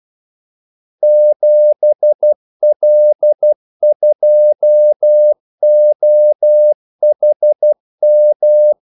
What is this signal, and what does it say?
Morse code, heard as a keyed tone, about 12 wpm, 600 Hz. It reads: 7L2OHM
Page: 2 ekstra ordmellemrum mellem kaldesignalerne